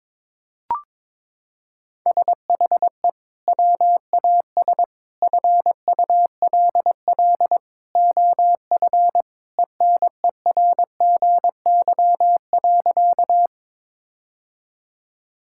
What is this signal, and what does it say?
Morse code, heard as a keyed tone, about 21 wpm, 700 Hz. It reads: SHE WAS FULL OF ENERGY.
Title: Sentences from Top 1000 Words - Review Part 6 - 22wpm